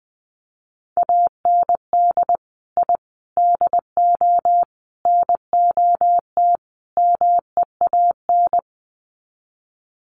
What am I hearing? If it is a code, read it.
Morse code, heard as a keyed tone, about 20 wpm, 700 Hz. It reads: AND I DO NOT MEAN
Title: Sets of 5 Words Encore2 20wpm